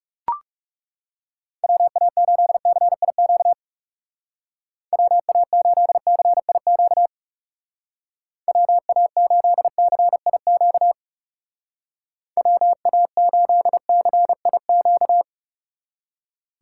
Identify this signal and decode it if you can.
Morse code, heard as a keyed tone, about 37 wpm, 700 Hz. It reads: WA8CIQ WA8CIQ WA8CIQ WA8CIQ
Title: Speed Racing - USA Call Signs 30wpm